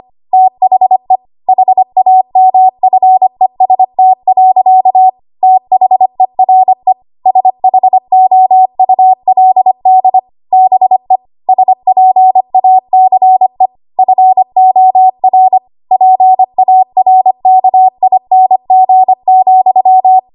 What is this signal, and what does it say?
Morse code, heard as a keyed tone, about 25 wpm, 750 Hz. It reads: THE HAMFEST. THERE SHOULD BE SPACE FOR PARKING,
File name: Morse / 25WPM